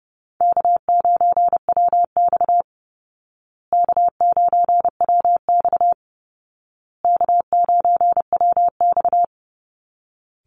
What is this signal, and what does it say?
Morse code, heard as a keyed tone, about 30 wpm, 700 Hz. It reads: K9WX K9WX K9WX
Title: Easy Call Signs Repeated 3x - 30wpm